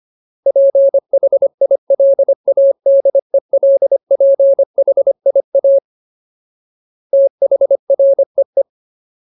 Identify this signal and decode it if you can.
Morse code, heard as a keyed tone, about 25 wpm, 550 Hz. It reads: PHILADELPHIA THREE
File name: Ultimate Word List / 25WPM Ultimate